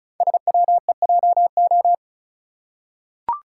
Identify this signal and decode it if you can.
Morse code, heard as a keyed tone, about 36 wpm, 690 Hz. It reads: SWEJO
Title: Code Groups - 5 Letters - 35wpm